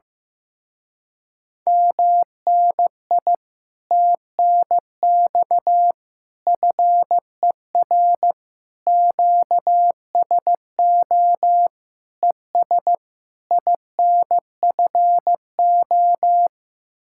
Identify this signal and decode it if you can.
Morse code, heard as a keyed tone, about 15 wpm, 700 Hz. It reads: MNI TNX FER QSO ES INFO
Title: Speed-Racing Common QSO Phrases Encore - 15wpm